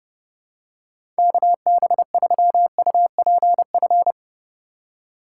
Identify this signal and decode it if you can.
Morse code, heard as a keyed tone, about 29 wpm, 700 Hz. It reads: KB3UPF
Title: USA Call Signs - 4x Character Spacing - 30wpm